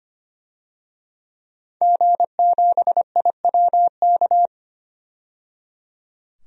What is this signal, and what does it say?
Morse code, heard as a keyed tone, about 25 wpm, 700 Hz. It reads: G7IWK
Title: DXCC-100 Call Signs - 5 Characters - 25wpm